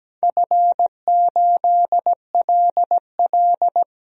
Morse code, heard as a keyed tone, about 17 wpm, 700 Hz. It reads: F8LL
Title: DXCC-100 Call Signs - 4 Characters - Repeated 3x - 17wpm